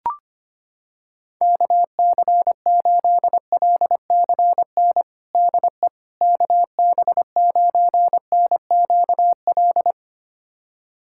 Morse code, heard as a keyed tone, about 25 wpm, 700 Hz. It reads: KC8LCN DE KB9NQL
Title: USA Callsign DE Callsign - 6 Characters - 25wpm